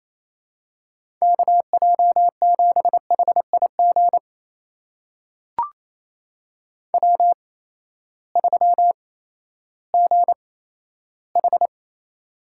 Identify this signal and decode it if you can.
Morse code, heard as a keyed tone, about 27 wpm, spaced as for 14 wpm, 700 Hz. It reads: KJ7HIG W3GH
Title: USA Call Signs - 8x Character Spacing - 28wpm